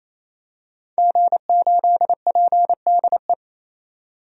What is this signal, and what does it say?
Morse code, heard as a keyed tone, about 28 wpm, 700 Hz. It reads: G8PDE
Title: DXCC-100 Call Signs - 5 Characters - 28wpm